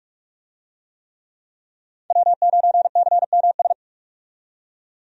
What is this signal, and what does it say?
Morse code, heard as a keyed tone, about 44 wpm, 700 Hz. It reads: W9CMS
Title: USA Call Signs - 5 Characters - 45wpm